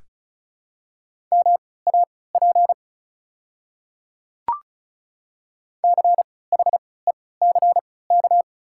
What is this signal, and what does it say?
Morse code, heard as a keyed tone, about 34 wpm, spaced as for 20 wpm, 700 Hz. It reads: MAP CHECK
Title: 3x Character Spacing - Top 500 Words - 35wpm